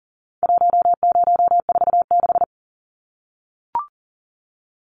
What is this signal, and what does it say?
Morse code, heard as a keyed tone, about 40 wpm, 700 Hz. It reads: 1046
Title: Numbers - 4-Digit 40wpm